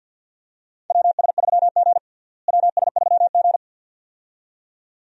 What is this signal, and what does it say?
Morse code, heard as a keyed tone, about 46 wpm, 700 Hz. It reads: WS3G WS3G
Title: USA Call Signs - 4 Characters - Repeated 2x - 50wpm